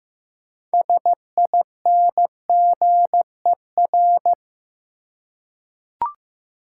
Morse code, heard as a keyed tone, about 15 wpm, 700 Hz. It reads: SINGER
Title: Six Letter Words - 15wpm